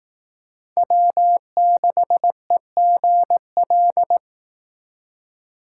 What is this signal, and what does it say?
Morse code, heard as a keyed tone, about 18 wpm, 700 Hz. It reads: W6EGL
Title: Speed Racing - USA 1x3 Call Signs 15wpm